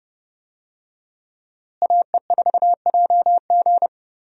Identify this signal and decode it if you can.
Morse code, heard as a keyed tone, about 30 wpm, 700 Hz. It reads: AE4JG
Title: USA Call Signs - 5 Characters - Repeated 3x - 30wpm